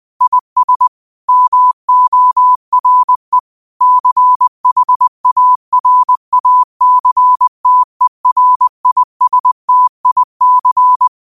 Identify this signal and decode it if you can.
Morse code, heard as a keyed tone, about 20 wpm, 1000 Hz. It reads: IS MORE CHARACTERISTIC
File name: 20wpm